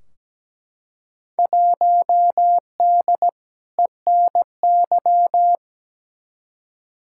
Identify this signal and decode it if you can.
Morse code, heard as a keyed tone, about 17 wpm, 700 Hz. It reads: 1D ENY
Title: Field Day Exchanges - 17wpm